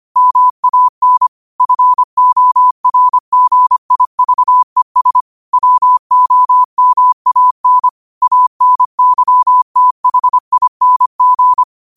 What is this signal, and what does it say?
Morse code, heard as a keyed tone, about 25 wpm, 1000 Hz. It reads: MAN FORGIVES WOMAN ANYTHING
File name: qotdfiles / 25wpm